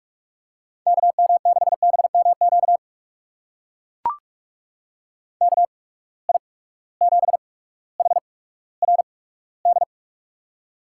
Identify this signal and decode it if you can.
Morse code, heard as a keyed tone, about 43 wpm, spaced as for 19 wpm, 700 Hz. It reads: KM6BMQ KI7HRD
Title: USA Call Signs - 6 Characters - 8x Character Spacing - 45wpm